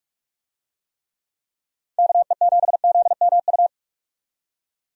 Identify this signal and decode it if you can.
Morse code, heard as a keyed tone, about 45 wpm, 700 Hz. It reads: KE7ZMU
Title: USA Call Signs - 6 Characters - 45wpm